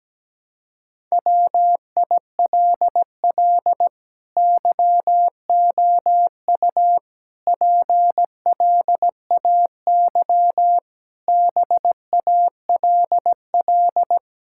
Morse code, heard as a keyed tone, about 17 wpm, 700 Hz. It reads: WILL YOU PLAY BALL
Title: Sentences from Top 1000 Words - Review Part 6 - 17wpm